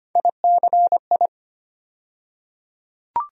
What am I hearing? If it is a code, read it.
Morse code, heard as a keyed tone, about 25 wpm, 700 Hz. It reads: ICI
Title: USA Call Sign Postfixes - 3 Letters 25wpm